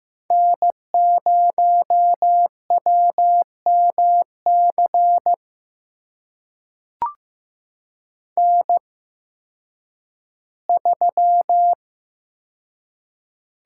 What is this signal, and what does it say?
Morse code, heard as a keyed tone, about 15 wpm, spaced as for 10 wpm, 700 Hz. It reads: N0WMC N3
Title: USA Call Signs - 5 Characters - 8x Character Spacing - 15wpm